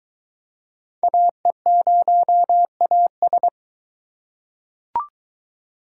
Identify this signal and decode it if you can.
Morse code, heard as a keyed tone, about 22 wpm, 700 Hz. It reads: AE0AS